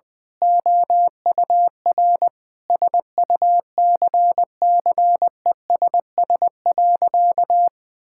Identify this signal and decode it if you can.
Morse code, heard as a keyed tone, about 20 wpm, 700 Hz. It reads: OUR SUCCESS.
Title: Sentences with 9-Word Phrases - 20wpm